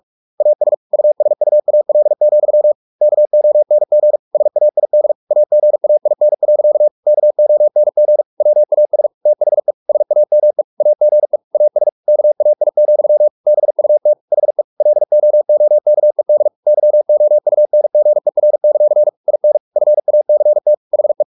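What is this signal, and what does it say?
Morse code, heard as a keyed tone, about 44 wpm, 600 Hz. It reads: AS USUAL, KONG SAID AGAIN. KONG WAS THE SAME AGE AS KAI, BUT HE LOOKED YOUNGER; IN FACT HE
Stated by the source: Morse practice at 45 wpm, the novel Never by Ken Follett in English, from CW med Gnister.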